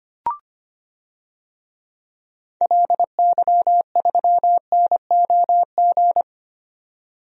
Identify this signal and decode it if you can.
Morse code, heard as a keyed tone, about 25 wpm, 700 Hz. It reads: LY3NOG
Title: DXCC-100 Call Signs - 6 Characters - 25wpm